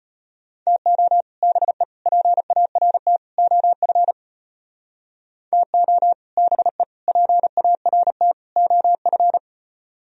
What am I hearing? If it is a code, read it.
Morse code, heard as a keyed tone, about 33 wpm, 700 Hz. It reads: TO BE PART OF TO BE PART OF